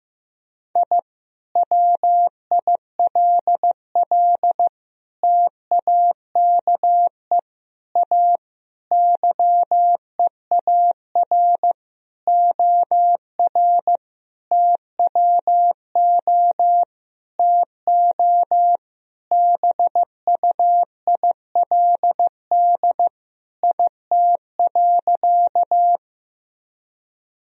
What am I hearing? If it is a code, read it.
Morse code, heard as a keyed tone, about 16 wpm, 700 Hz. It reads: I WILL TAKE A YEAR OR TWO TO BUILD IT.
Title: Sentences from Top 1000 Words - Review Part 1 - 15wpm